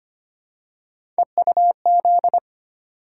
Morse code, heard as a keyed tone, about 25 wpm, 700 Hz. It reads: EUZ
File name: Three Letter-Number Combinations - 25wpm